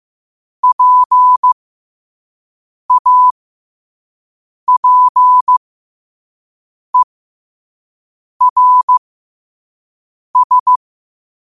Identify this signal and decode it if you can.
Morse code, heard as a keyed tone, about 15 wpm, spaced as for 5 wpm, 1000 Hz. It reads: PAPERS
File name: qotdfiles / 5wpm